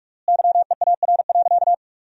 Morse code, heard as a keyed tone, about 42 wpm, 700 Hz. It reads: YEAR.
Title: 3x Word-Spacing - Sentences from Top 300 Words - 45wpm